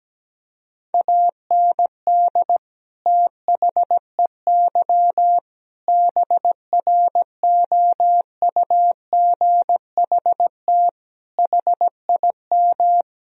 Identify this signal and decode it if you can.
Morse code, heard as a keyed tone, about 17 wpm, 700 Hz. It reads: AND THEY BROUGHT HIM
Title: Sentences from Top 1000 Words - Review Part 2 - 17wpm